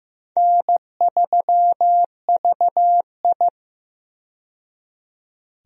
Morse code, heard as a keyed tone, about 15 wpm, 700 Hz. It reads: N3VI
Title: USA Call Signs - 4 Characters - 15wpm